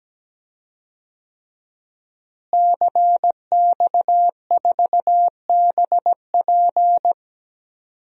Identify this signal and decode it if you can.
Morse code, heard as a keyed tone, about 17 wpm, 700 Hz. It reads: CX4BP
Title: DXCC-100 Call Signs - 5 Characters - Repeated 3x - 17wpm